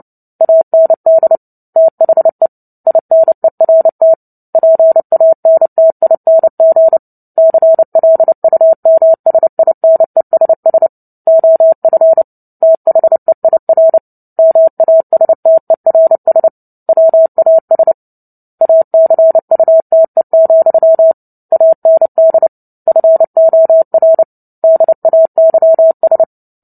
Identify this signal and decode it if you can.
Morse code, heard as a keyed tone, about 30 wpm, 650 Hz. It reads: AND THE INERT PANTING CLUMSINESS OF THEIR MASTERS WAS ACUTE, AND FOR DAYS
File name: Book Two / 03-Chapter 3 - 29 WPM